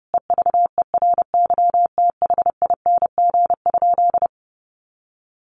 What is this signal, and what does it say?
Morse code, heard as a keyed tone, about 30 wpm, 700 Hz. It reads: EVERYTHING?
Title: Sentences from Top 700 Words 30wpm